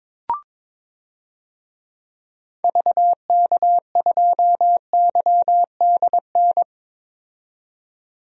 Morse code, heard as a keyed tone, about 22 wpm, 700 Hz. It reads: VK2YDN
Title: DXCC-100 Call Signs - 6 Characters - 22wpm